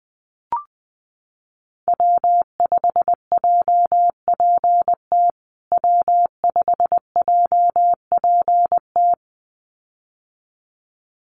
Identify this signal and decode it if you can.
Morse code, heard as a keyed tone, about 20 wpm, 700 Hz. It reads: W5JPT W5JPT
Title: USA Call Signs Repeated 2x 20wpm